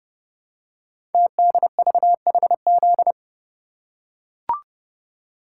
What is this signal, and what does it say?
Morse code, heard as a keyed tone, about 30 wpm, 700 Hz. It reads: TDVHZ
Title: Code Groups - 5 Letters - 30wpm